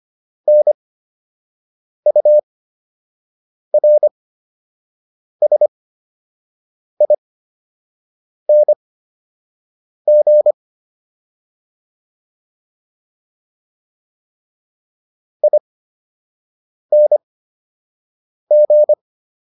Morse code, heard as a keyed tone, about 25 wpm, spaced as for 6 wpm, 600 Hz. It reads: NURSING ING